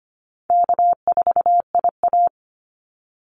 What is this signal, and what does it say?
Morse code, heard as a keyed tone, about 25 wpm, 700 Hz. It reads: K4IA
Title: Easy Call Signs - 25wpm